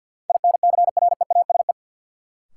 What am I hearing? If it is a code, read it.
Morse code, heard as a keyed tone, about 53 wpm, 700 Hz. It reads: INCREASE
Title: Sets of 5 Words - New Word Drill - 50wpm